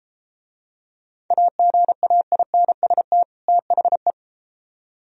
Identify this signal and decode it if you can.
Morse code, heard as a keyed tone, about 33 wpm, 700 Hz. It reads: AGAINST THE